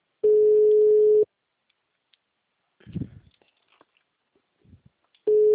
Ringback tone (Europe, 425 Hz).
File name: out- -2016